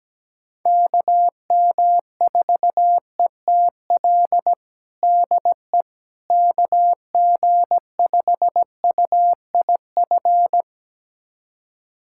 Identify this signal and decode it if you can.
Morse code, heard as a keyed tone, about 17 wpm, 700 Hz. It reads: KM4ETL DE KG5UIF
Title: USA Callsign DE Callsign - 6 Characters - 17wpm